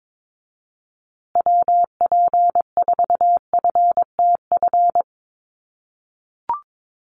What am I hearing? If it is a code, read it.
Morse code, heard as a keyed tone, about 22 wpm, 700 Hz. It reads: WP4FTF
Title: USA Call Signs Repeated 3x 22wpm